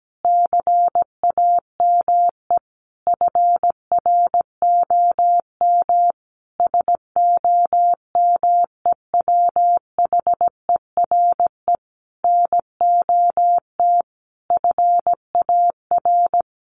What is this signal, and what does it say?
Morse code, heard as a keyed tone, about 17 wpm, 700 Hz. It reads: CAME FROM SOMEWHERE NOT FAR
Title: Sentences from Top 1400 Words 17wpm